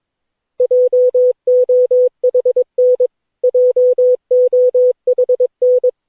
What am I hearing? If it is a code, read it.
Morse code, heard as a keyed tone, about 22 wpm, 500 Hz. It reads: JOHN JOHN
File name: cw / Session9 cw training